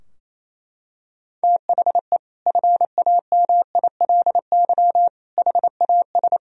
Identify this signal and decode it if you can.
Morse code, heard as a keyed tone, about 28 wpm, 700 Hz. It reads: THE FAMILY HAS
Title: Sentences from Top 1000 Words - Review Part 2 - 28wpm